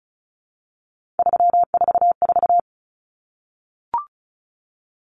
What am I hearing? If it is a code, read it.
Morse code, heard as a keyed tone, about 34 wpm, 700 Hz. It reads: 344
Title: Numbers - 3-Digit 35wpm